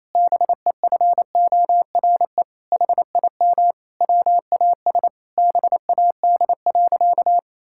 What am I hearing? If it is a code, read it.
Morse code, heard as a keyed tone, about 28 wpm, 700 Hz. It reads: BEFORE HIM WAS BAD.
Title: Sentences from Top 1600 Words 28wpm